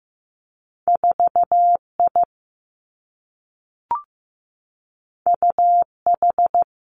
Morse code, heard as a keyed tone, about 15 wpm, 700 Hz. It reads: 4I UH
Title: Two Challenging Letter-Number Combinations 15wpm